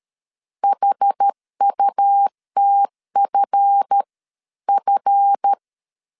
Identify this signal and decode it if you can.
Morse code, heard as a keyed tone, about 13 wpm, spaced as for 11 wpm, 800 Hz. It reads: HUTFF
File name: LEZIONI / 7 GF9Z / 7 GF 12wpm 2122